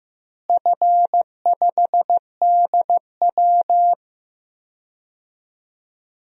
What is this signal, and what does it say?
Morse code, heard as a keyed tone, about 16 wpm, 700 Hz. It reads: F5DW
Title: DXCC-100 Call Signs - 4 Characters - Repeated 3x - 15wpm